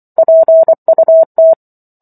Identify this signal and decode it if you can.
Morse code, heard as a keyed tone, about 24 wpm, 650 Hz. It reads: PUT